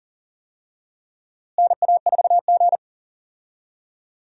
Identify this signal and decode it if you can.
Morse code, heard as a keyed tone, about 40 wpm, 700 Hz. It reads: NA4G
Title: USA Call Signs - 4 Characters - Repeated 3x - 40wpm